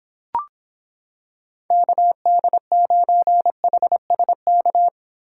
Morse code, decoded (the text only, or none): KD9HSK